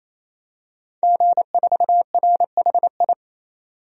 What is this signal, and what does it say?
Morse code, heard as a keyed tone, about 28 wpm, 700 Hz. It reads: G4RHI
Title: DXCC-100 Call Signs - 5 Characters - 28wpm